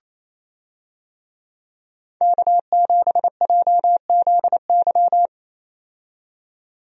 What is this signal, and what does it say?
Morse code, heard as a keyed tone, about 28 wpm, 700 Hz. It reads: K7JZY
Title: USA Call Signs - 5 Characters - 28wpm